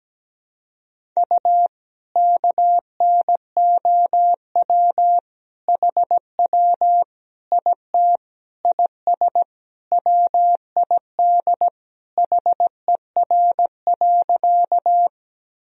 Morse code, heard as a keyed tone, about 17 wpm, 700 Hz. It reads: U KNOW HW IT IS WID HER.
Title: Sentences from Top 100 Words and Abbreviations 17wpm